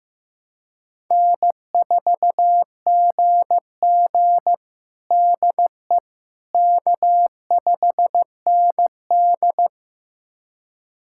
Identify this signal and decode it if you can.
Morse code, heard as a keyed tone, about 15 wpm, 700 Hz. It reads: N4GG DE K5ND
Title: USA Callsign DE Callsign - 4 Characters - 15wpm